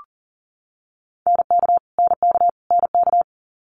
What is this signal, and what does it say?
Morse code, heard as a keyed tone, about 40 wpm, 690 Hz. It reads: NK NK NK
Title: Multi-letter Phonemes Repeated 3x 40wpm